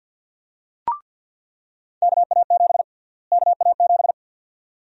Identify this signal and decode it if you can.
Morse code, heard as a keyed tone, about 46 wpm, 700 Hz. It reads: KA7 KA7
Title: USA Call Sign Prefixes - Repeated 2x - 50wpm